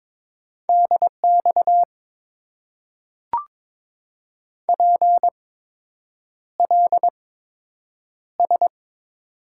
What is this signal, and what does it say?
Morse code, heard as a keyed tone, about 22 wpm, spaced as for 8 wpm, 700 Hz. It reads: DX PLS
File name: Top 90 QSO Elements - 8x Character Spacing - 22wpm